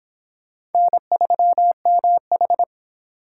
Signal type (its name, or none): Morse code